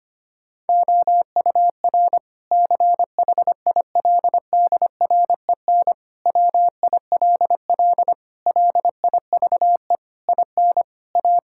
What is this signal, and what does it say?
Morse code, heard as a keyed tone, about 25 wpm, 700 Hz. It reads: OUR CHILDREN WILL LIVE IN A
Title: Sentences from Top 1000 Words - Review Part 5 - 25wpm